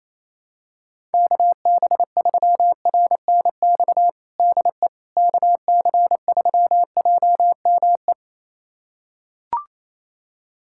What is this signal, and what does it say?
Morse code, heard as a keyed tone, about 28 wpm, 700 Hz. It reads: KB3RNX DE KC3JME
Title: USA Callsign DE Callsign 28wpm